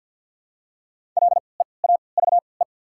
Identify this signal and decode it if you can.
Morse code, heard as a keyed tone, about 53 wpm, spaced as for 29 wpm, 690 Hz. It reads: LEAVE
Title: 3x Character Spacing - Top 500 Words - 50wpm